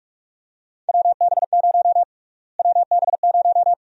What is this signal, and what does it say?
Morse code, heard as a keyed tone, about 44 wpm, 700 Hz. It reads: WB0 WB0
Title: USA Call Sign Prefixes - Repeated 2x - 45wpm